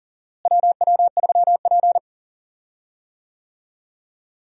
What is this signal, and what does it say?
Morse code, heard as a keyed tone, about 40 wpm, 700 Hz. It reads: WW3P
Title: USA Call Signs - 4 Characters - Repeated 3x - 40wpm